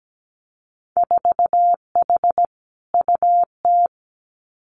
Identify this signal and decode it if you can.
Morse code, heard as a keyed tone, about 17 wpm, 700 Hz. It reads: 4H UT
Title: Winter Field Day Exchanges Encore 17wpm